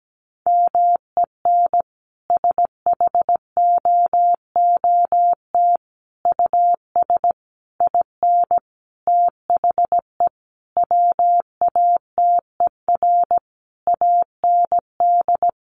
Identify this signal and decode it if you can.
Morse code, heard as a keyed tone, about 17 wpm, 700 Hz. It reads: MEN SHOOT US IN THE WATER AND
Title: Sentences from Top 1200 Words 17wpm